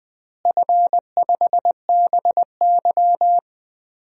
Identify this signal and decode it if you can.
Morse code, heard as a keyed tone, about 20 wpm, 700 Hz. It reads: F5BY
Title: DXCC-100 Call Signs - 4 Characters - Repeated 3x - 20wpm